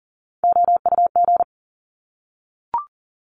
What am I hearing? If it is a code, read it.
Morse code, heard as a keyed tone, about 40 wpm, 700 Hz. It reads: OUG
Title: Top Three Letter Combinations in Words 40wpm